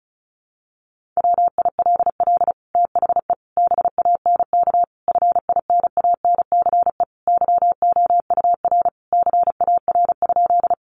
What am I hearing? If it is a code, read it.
Morse code, heard as a keyed tone, about 34 wpm, 700 Hz. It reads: WILL THE BANK FINANCE YOUR CAR?
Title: Sentences from Top 1500 Words 35wpm